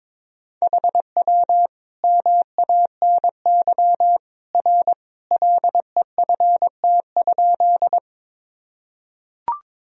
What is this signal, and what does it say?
Morse code, heard as a keyed tone, about 22 wpm, 700 Hz. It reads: HW MANY R LEFT?
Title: Sentences from Top 300 Words and Abbreviations 22wpm